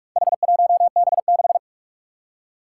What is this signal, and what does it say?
Morse code, heard as a keyed tone, about 44 wpm, 700 Hz. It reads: H1B6
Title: Four Challenging Letter-Number Combinations 45wpm